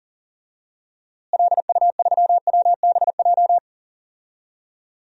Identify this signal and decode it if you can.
Morse code, heard as a keyed tone, about 40 wpm, 700 Hz. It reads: LU3WBJ
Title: DXCC-100 Call Signs - 6 Characters - Repeated 3x - 40wpm